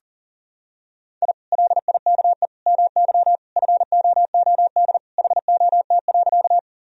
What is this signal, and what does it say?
Morse code, heard as a keyed tone, about 40 wpm, 700 Hz. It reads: I LIKE MY FOOD HOT.
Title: Sentences from Top 1000 Words - Review Part 3 - 40wpm